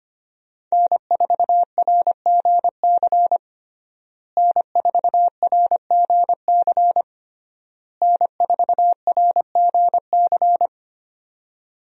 Morse code, heard as a keyed tone, about 25 wpm, 700 Hz. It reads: N4RGC N4RGC N4RGC
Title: USA Call Signs - 5 Characters - Repeated 3x - 25wpm